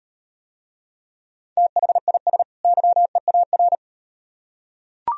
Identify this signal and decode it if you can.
Morse code, heard as a keyed tone, about 38 wpm, 700 Hz. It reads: THIS YEAR